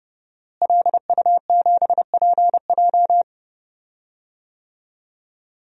Morse code, heard as a keyed tone, about 30 wpm, 700 Hz. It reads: LU7PJ
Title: DXCC-100 Call Signs - 5 Characters - Repeated 3x - 30wpm